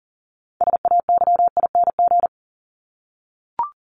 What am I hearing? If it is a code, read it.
Morse code, heard as a keyed tone, about 40 wpm, 700 Hz. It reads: SAYING
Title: ING Suffix - Top 100 Words with Base Words 40wpm